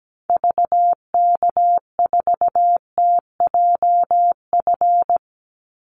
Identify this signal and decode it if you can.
Morse code, heard as a keyed tone, about 17 wpm, 700 Hz. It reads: VK4TJF
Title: Call Signs Repeated 3x - 17wpm